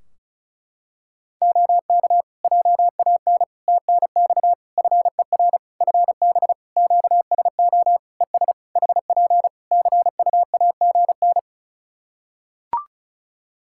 Morse code, heard as a keyed tone, about 36 wpm, 700 Hz. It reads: OK JAN TNX FER FB QSO ES HP CUAGN
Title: K1USN SST QSO Practice 35wpm